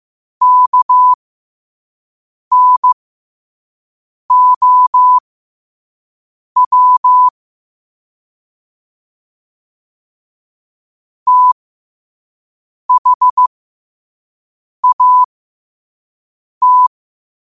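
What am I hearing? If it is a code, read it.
Morse code, heard as a keyed tone, about 15 wpm, spaced as for 5 wpm, 1000 Hz. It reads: KNOW THAT
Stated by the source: Morse practice quote sent at 5 wpm.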